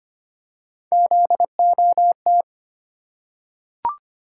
Morse code, heard as a keyed tone, about 25 wpm, 700 Hz. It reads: ZOT
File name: Three Letter-Number Combinations - 25wpm